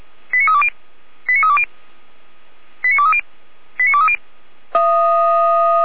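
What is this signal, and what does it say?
Five-tone selective call (ZVEI), heard as quick runs of five tones, tones of about 70 ms. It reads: 89320 89320 89230 89230
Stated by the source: ZVEI five-tone selective call received on an SDR, archived by fire brigade Orlamünde (feuerwehr-orlamuende).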